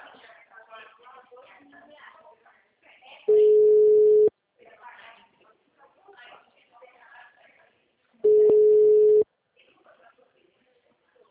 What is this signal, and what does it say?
Ringback tone (Europe, 425 Hz), with speech.
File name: out- -2009- - - .148